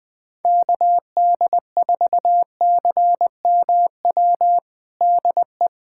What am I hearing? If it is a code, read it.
Morse code, heard as a keyed tone, about 20 wpm, 700 Hz. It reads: KD4CMW DE
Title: USA Callsign DE Callsign 20wpm